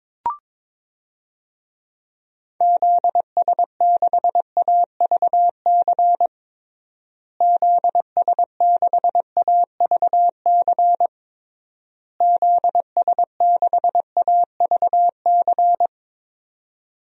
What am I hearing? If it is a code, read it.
Morse code, heard as a keyed tone, about 21 wpm, 700 Hz. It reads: ZS6AVC ZS6AVC ZS6AVC
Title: DXCC-100 Call Signs - 6 Characters - Repeated 3x - 22wpm